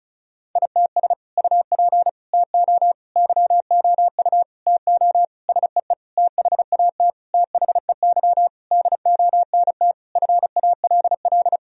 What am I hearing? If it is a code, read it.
Morse code, heard as a keyed tone, about 36 wpm, 700 Hz. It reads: ITS UP TO YOU TO SEE THAT THEY DONT FALL
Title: Sentences from Top 1000 Words - Review Part 2 - 35wpm